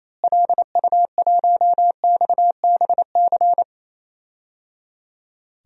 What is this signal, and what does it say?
Morse code, heard as a keyed tone, about 27 wpm, 700 Hz. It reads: LU1XBC
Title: DXCC-100 Call Signs - 6 Characters - Repeated 3x - 28wpm